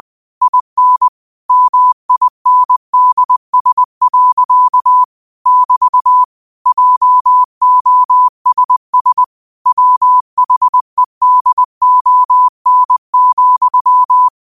Morse code, heard as a keyed tone, about 20 wpm, 1000 Hz. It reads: IN MINDS. <BT> JOSS WHEDON,
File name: qotdfiles / 20wpm